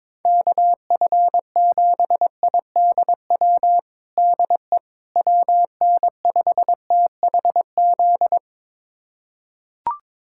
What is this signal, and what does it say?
Morse code, heard as a keyed tone, about 22 wpm, 700 Hz. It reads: KF7IDW DE WN5THZ
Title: USA Callsign DE Callsign - 6 Characters - 22wpm